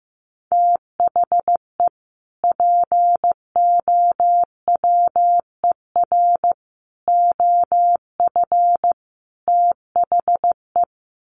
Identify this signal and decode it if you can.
Morse code, heard as a keyed tone, about 15 wpm, 700 Hz. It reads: THE POWER OF THE
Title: Sets of 4 Words 15wpm